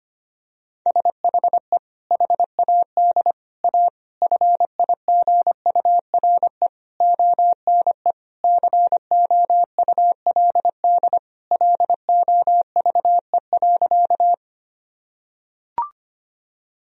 Morse code, heard as a keyed tone, about 25 wpm, 700 Hz. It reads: SHE HAD A FIGURE ONE COULD LOVE.
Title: Sentences from Top 1000 Words - Review Part 2 - 25wpm